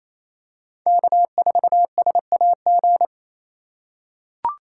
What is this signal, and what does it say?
Morse code, heard as a keyed tone, about 28 wpm, 700 Hz. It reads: K4SAG
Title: USA Call Signs - 4x Character Spacing - 28wpm